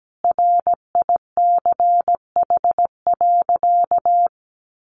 Morse code, heard as a keyed tone, about 17 wpm, 700 Hz. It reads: RICH.
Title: Sentences from Top 1100 Words 17wpm